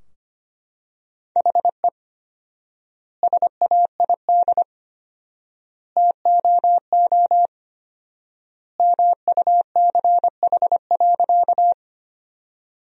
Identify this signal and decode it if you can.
Morse code, heard as a keyed tone, about 25 wpm, 700 Hz. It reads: HE SAID TOO MUCH.
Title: 4x Word-Spacing - Sentences from Top 200 Words - 25wpm